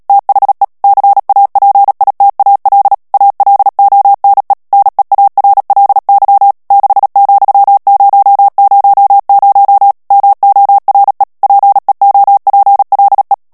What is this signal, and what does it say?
Morse code, heard as a keyed tone, about 37 wpm, 800 Hz. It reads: THE CAPITAL ALONE NEARLY 6,000 MORE PEOPLE
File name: morse mp3 / ANGLAIS texte clair 2 vitesse 35 mots minute